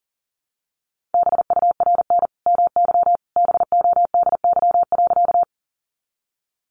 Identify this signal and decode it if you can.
Morse code, heard as a keyed tone, about 40 wpm, 700 Hz. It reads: BURN MY BODY.
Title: Sentences from Top 1300 Words 40wpm